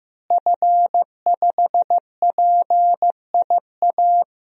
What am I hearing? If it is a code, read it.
Morse code, heard as a keyed tone, about 15 wpm, 700 Hz. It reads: F5PIA
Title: DXCC-100 Call Signs - 5 Characters - Repeated 3x - 15wpm